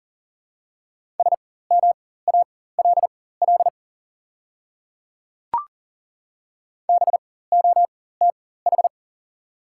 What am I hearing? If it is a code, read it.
Morse code, heard as a keyed tone, about 37 wpm, spaced as for 18 wpm, 700 Hz. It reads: SMALL BOTH
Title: 4x Character Spacing - Top 300 Words - 40wpm